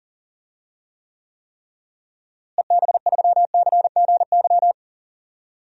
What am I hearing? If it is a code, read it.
Morse code, heard as a keyed tone, about 40 wpm, 700 Hz. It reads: EB3CGY
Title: DXCC-100 Call Signs - 6 Characters - Repeated 3x - 40wpm